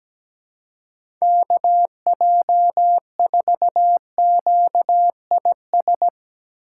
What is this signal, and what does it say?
Morse code, heard as a keyed tone, about 17 wpm, 700 Hz. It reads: KJ4QIS
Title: USA Call Signs - 4x Character Spacing - 17wpm